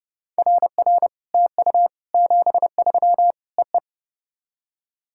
Morse code, heard as a keyed tone, about 29 wpm, 700 Hz. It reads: RR TU 73 EE
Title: SOTA QSO Practice 30wpm